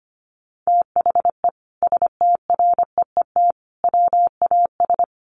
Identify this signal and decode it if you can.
Morse code, heard as a keyed tone, about 25 wpm, 700 Hz. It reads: THE STREET WAS
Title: Sentences from Top 1300 Words 25wpm